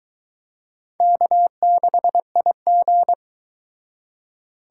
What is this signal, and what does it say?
Morse code, heard as a keyed tone, about 22 wpm, 700 Hz. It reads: K6IG